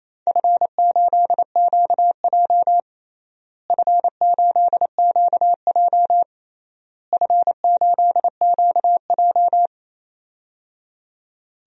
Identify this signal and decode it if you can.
Morse code, heard as a keyed tone, about 27 wpm, 700 Hz. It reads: F8QJ F8QJ F8QJ
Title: DXCC-100 Call Signs - 4 Characters - Repeated 3x - 28wpm